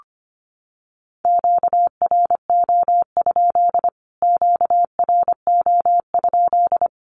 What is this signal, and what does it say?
Morse code, heard as a keyed tone, about 25 wpm, 700 Hz. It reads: QRO? QRO?
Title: Q-Codes Repeated 3x 25wpm